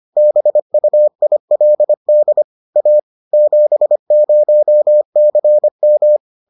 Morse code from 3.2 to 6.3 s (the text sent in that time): 70CM